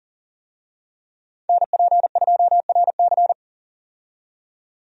Morse code, heard as a keyed tone, about 40 wpm, 700 Hz. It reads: NP2RC
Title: USA Call Signs - 5 Characters - 40wpm